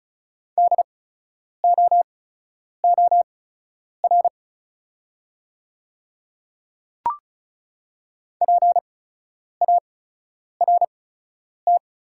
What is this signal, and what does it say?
Morse code, heard as a keyed tone, about 36 wpm, spaced as for 10 wpm, 690 Hz. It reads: DOOR PART
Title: 8x Character Spacing - Top 500 Words - 35wpm